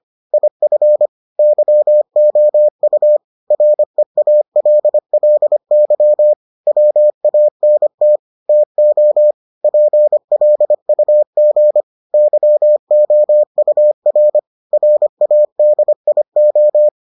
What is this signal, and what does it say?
Morse code, heard as a keyed tone, about 25 wpm, 600 Hz. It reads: IF YOU REALLY WANT TO PLUG YOUR RADIO